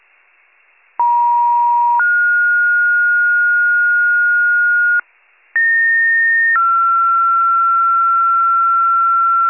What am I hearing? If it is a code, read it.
Two-tone page, heard as a one-second tone, then a three-second tone, about 950 then 1430 Hz, about 1850 then 1360 Hz. It is heard from 1.0 to 9.5 s.